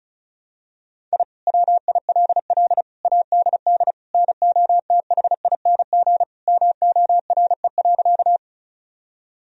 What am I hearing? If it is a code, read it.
Morse code, heard as a keyed tone, about 36 wpm, 700 Hz. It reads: I WILL ADD NOTHING MORE.
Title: Sentences from Top 1000 Words - Review Part 1 - 35wpm